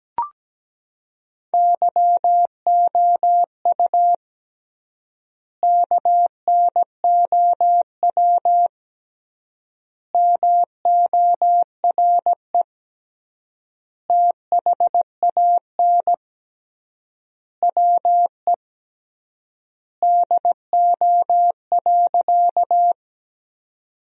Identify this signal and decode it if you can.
Morse code, heard as a keyed tone, about 17 wpm, 700 Hz. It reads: YOU KNOW MORE THAN WE DO.
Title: 3x Word-Spacing - Sentences from Top 100 Words - 17wpm